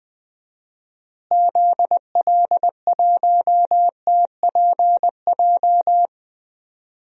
Morse code, heard as a keyed tone, about 20 wpm, 700 Hz. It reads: ZL1TPJ
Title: DXCC-100 Call Signs - 6 Characters - Repeated 3x - 20wpm